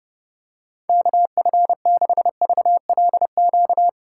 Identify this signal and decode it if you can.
Morse code, heard as a keyed tone, about 30 wpm, 700 Hz. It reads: KF6VLQ
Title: USA Call Signs - 6 Characters - 8x Character Spacing - 30wpm